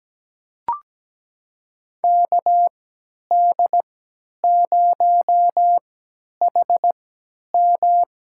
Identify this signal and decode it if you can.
Morse code, heard as a keyed tone, about 17 wpm, spaced as for 10 wpm, 700 Hz. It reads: KD0HM
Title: USA Call Signs - 5 Characters - 3x Character Spacing - 17wpm